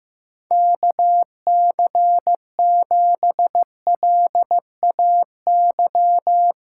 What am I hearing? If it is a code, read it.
Morse code, heard as a keyed tone, about 16 wpm, 700 Hz. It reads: KC7LAY
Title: USA Callsign DE Callsign - 6 Characters - 15wpm